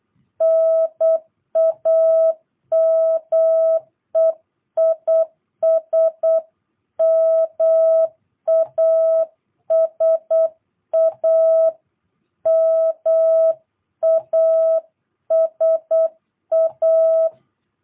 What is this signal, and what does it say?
Morse code, heard as a keyed tone, about 8 wpm, 650 Hz. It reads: NAME IS MASA MASA
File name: jh8cbh / 2022cw